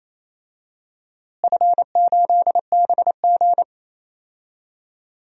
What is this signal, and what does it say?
Morse code, heard as a keyed tone, about 28 wpm, 700 Hz. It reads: F8BG
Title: DXCC-100 Call Signs - 4 Characters - Repeated 2x - 28wpm